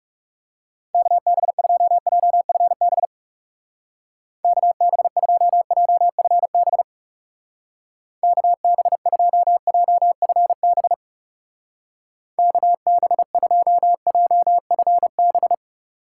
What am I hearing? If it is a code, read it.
Morse code, heard as a keyed tone, about 38 wpm, 700 Hz. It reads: KB2JFB KB2JFB KB2JFB KB2JFB